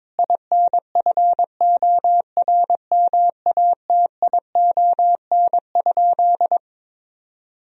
Morse code, heard as a keyed tone, about 21 wpm, 700 Hz. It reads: INFORMATION?
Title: Sentences from Top 1000 Words - Review Part 3 - 22wpm